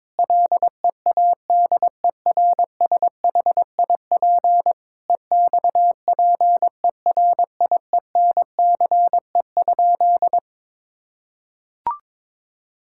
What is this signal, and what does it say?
Morse code, heard as a keyed tone, about 21 wpm, 700 Hz. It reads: LEADERSHIP EXPERIENCE?
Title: Sentences from Top 1600 Words 22wpm